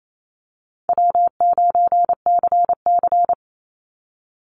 Morse code, heard as a keyed tone, about 28 wpm, 700 Hz. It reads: W9CC
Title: Easy Call Signs - 28wpm